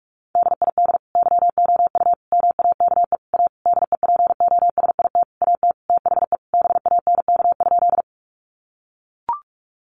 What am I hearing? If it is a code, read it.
Morse code, heard as a keyed tone, about 44 wpm, 700 Hz. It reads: DID YOU MAKE A DEPOSIT AT THE BANK?
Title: Sentences from Top 1300 Words 45wpm